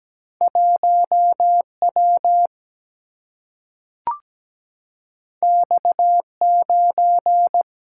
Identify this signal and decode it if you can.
Morse code, heard as a keyed tone, about 17 wpm, 700 Hz. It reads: 1W X9
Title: Single-Letter Single-Number 17wpm